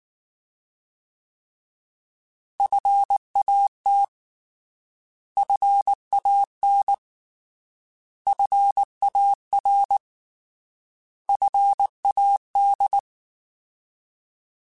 Morse code, heard as a keyed tone, about 19 wpm, 800 Hz. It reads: FAT FAN FAR FAD
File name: Alphabetical / Alphabetical mp3s / Fs @18